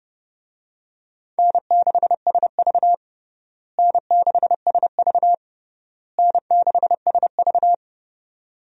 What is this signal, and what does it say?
Morse code, heard as a keyed tone, about 32 wpm, 700 Hz. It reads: N6SV N6SV N6SV
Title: USA Call Signs - 4 Characters - Repeated 3x - 30wpm